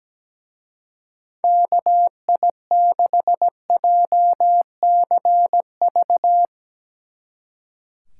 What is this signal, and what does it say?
Morse code, heard as a keyed tone, about 17 wpm, 700 Hz. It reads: KI6JCV
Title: Speed Racing - USA 2x3 Call Signs 17wpm